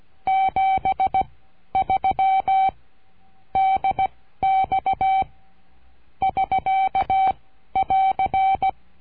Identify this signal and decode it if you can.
Morse code, heard as a keyed tone, about 16 wpm, spaced as for 12 wpm, 770 Hz. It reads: 73 DX <SK><AR>
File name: cursos / CURSO DE TELEGRAFIA / Telegrafia - Trilha 22